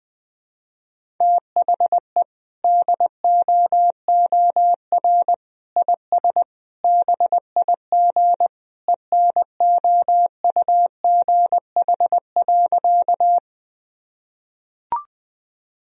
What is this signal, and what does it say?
Morse code, heard as a keyed tone, about 20 wpm, 700 Hz. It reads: THE DOOR IS BIG ENOUGH.
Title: Sentences from Top 1000 Words - Review Part 1 - 20wpm